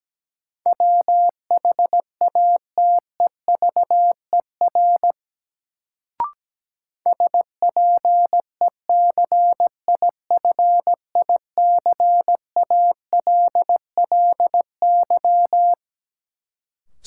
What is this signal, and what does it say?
Morse code, heard as a keyed tone, about 17 wpm, 700 Hz. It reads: WHATEVER SPECIFICALLY
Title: Sentences from Top 1100 Words - New Word Drill Alt - 17wpm